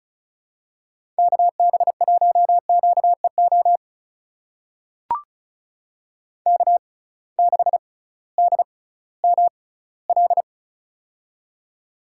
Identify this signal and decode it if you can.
Morse code, heard as a keyed tone, about 36 wpm, spaced as for 19 wpm, 700 Hz. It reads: KB1QEO K6DML
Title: USA Call Signs - 6x Character Spacing - 35wpm